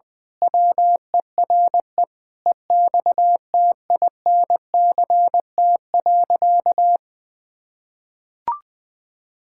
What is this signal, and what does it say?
Morse code, heard as a keyed tone, about 21 wpm, 700 Hz. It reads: WERE EXTINCT.
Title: Pangram Challenge v2 - 20wpm